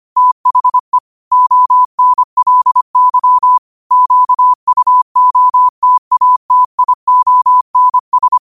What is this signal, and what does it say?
Morse code, heard as a keyed tone, about 25 wpm, 1000 Hz. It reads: THE ONLY QUOTATIONS
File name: qotdfiles / 25wpm